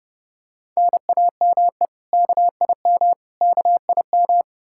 Morse code, heard as a keyed tone, about 31 wpm, 700 Hz. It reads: NAME KIM KIM
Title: K1USN SST QSO Practice 30wpm